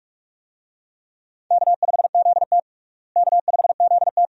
Morse code, heard as a keyed tone, about 47 wpm, 700 Hz. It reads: K5ZT K5ZT
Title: USA Call Signs - 4 Characters - Repeated 2x - 45wpm